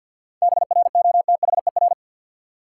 Morse code, heard as a keyed tone, about 50 wpm, 700 Hz. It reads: BROTHER
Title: Seven Letter Words - 50wpm